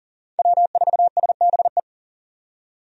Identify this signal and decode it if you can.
Morse code, heard as a keyed tone, about 40 wpm, 700 Hz. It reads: W4SBE
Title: USA Call Signs - 5 Characters - 40wpm